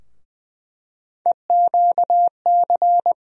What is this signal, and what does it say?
Morse code, heard as a keyed tone, about 20 wpm, 700 Hz. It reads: EQC
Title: USA Call Sign Postfixes - 3 Letters 20wpm